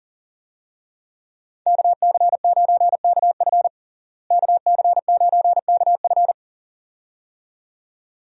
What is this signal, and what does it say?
Morse code, heard as a keyed tone, about 40 wpm, 700 Hz. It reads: KC9KF KC9KF
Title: USA Call Signs - 5 Characters - Repeated 2x - 40wpm